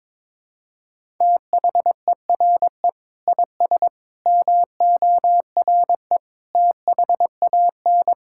Morse code, heard as a keyed tone, about 21 wpm, 700 Hz. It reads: THERE IS MORE THAN
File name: practice / 22wpm / Sets of 7 Words - 22wpm